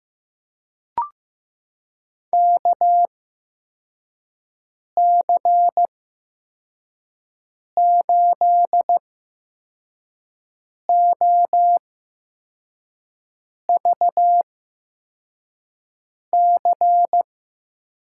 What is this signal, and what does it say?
Morse code, heard as a keyed tone, about 15 wpm, spaced as for 4 wpm, 700 Hz. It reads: KC8OVC
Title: USA Call Signs - 6 Characters - 8x Character Spacing - 15wpm